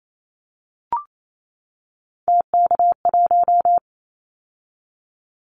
Morse code, heard as a keyed tone, about 28 wpm, 700 Hz. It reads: TK1
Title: Callsign Prefixes for Top 100 DXCC - 28wpm